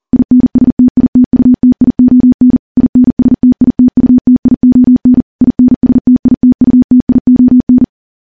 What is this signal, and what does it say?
Morse code, heard as a keyed tone, about 41 wpm, 250 Hz. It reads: INSTITUTION INSTITUTION INSTITUTION
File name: morse / morse-training-40WPM